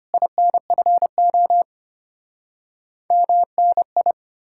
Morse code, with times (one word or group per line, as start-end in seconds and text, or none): 0.1-1.6 INFO
3.1-4.1 MNI